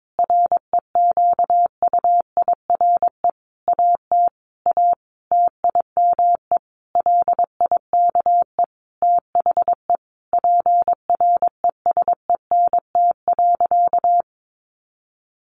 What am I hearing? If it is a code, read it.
Morse code, heard as a keyed tone, about 22 wpm, 700 Hz. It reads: REQUIRE AT A TIME LIKE THE PRESENT.
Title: Sentences from Top 900 Words 22wpm